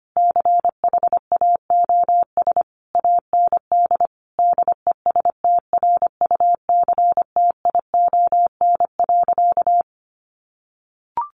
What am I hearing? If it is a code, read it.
Morse code, heard as a keyed tone, about 25 wpm, 700 Hz. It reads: CHAOS AND DESTRUCTION.
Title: A Sound Of Thunder 25wpm